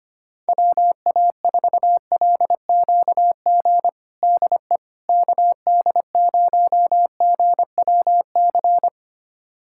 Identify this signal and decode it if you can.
Morse code, heard as a keyed tone, about 25 wpm, 700 Hz. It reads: WA4LQG DE KD0GWC
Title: USA Callsign DE Callsign - 6 Characters - 25wpm